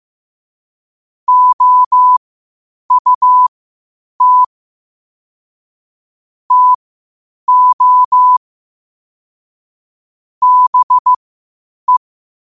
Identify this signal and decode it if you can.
Morse code, heard as a keyed tone, about 15 wpm, spaced as for 8 wpm, 1000 Hz. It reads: OUT TO BE
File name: qotdfiles / 8wpm